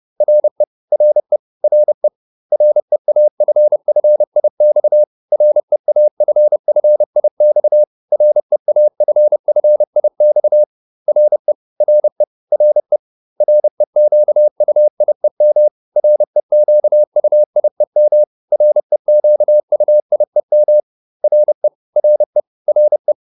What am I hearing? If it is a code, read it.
Morse code, heard as a keyed tone, about 30 wpm, 600 Hz. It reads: RE RE RE REAFFIX REAFFIX REAFFIX RE RE RE REQUIEM REQUIEM REQUIEM RE RE RE